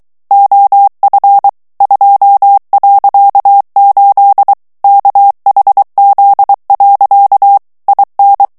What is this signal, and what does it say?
Morse code, heard as a keyed tone, about 23 wpm, 800 Hz. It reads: OF 2.8 KHZ. IN
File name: ANGLAIS texte clair 3 vitesse 22 mots minute